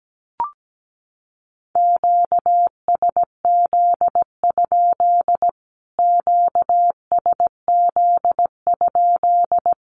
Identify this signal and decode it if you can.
Morse code, heard as a keyed tone, about 17 wpm, 700 Hz. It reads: QSZ? QSZ?
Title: Q-Codes Repeated 3x 17wpm